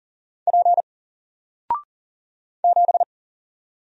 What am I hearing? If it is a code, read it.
Morse code, heard as a keyed tone, about 40 wpm, 700 Hz. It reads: P 7
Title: Single Character (Letter:Number:Punctuation:Prosign) 40wpm